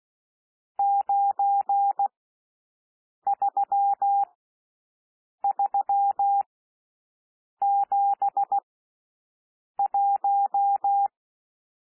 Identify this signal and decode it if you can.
Morse code, heard as a keyed tone, about 16 wpm, spaced as for 6 wpm, 800 Hz. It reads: 93371
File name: SOUND NUMBERS 16 wpm